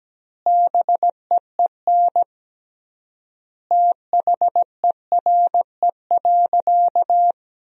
Morse code, heard as a keyed tone, about 17 wpm, 700 Hz. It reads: BEEN THERE.
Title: 3x Word-Spacing - Sentences from Top 300 Words - 17wpm